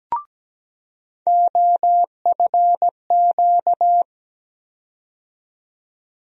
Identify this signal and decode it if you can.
Morse code, heard as a keyed tone, about 17 wpm, 700 Hz. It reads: OFQ
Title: USA Call Sign Postfixes - 3 Letters 17wpm